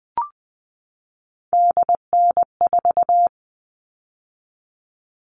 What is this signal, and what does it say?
Morse code, heard as a keyed tone, about 20 wpm, 700 Hz. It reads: DN4
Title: All Cut Numbers - 3-Digit 20wpm